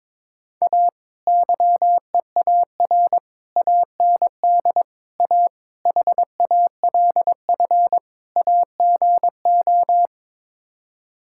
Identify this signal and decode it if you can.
Morse code, heard as a keyed tone, about 22 wpm, 700 Hz. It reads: A YEAR AND A HALF AGO
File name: practice / 22wpm / Sets of 6 Words - 22wpm